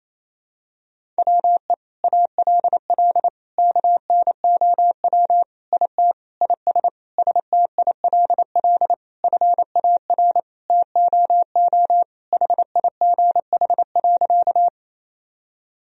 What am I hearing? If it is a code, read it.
Morse code, heard as a keyed tone, about 28 wpm, 700 Hz. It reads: WE ALL KNOW IT IS STILL FAR TOO HIGH.
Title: Sentences from Top 1000 Words - Review Part 1 - 28wpm